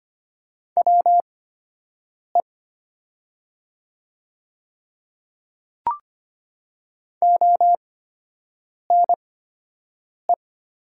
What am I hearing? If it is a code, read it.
Morse code, heard as a keyed tone, about 25 wpm, spaced as for 7 wpm, 700 Hz. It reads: WE ONE
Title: 8x Character Spacing - Top 100 Words - 25wpm